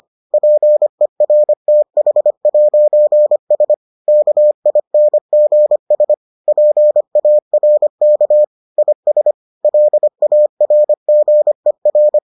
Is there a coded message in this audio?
Yes: Morse code from 0.3 to 12.2 s, heard as a keyed tone, about 25 wpm, 600 Hz. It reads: PERTH'S KINGS PARK IS LARGER